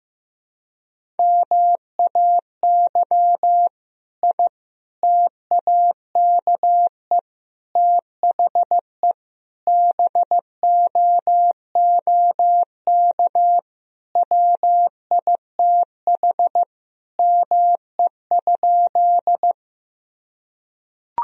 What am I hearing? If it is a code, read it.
Morse code, heard as a keyed tone, about 16 wpm, 700 Hz. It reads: MAY I TAKE THE BOOK WITH ME?
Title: Sentences from Top 1000 Words - Review Part 1 - 15wpm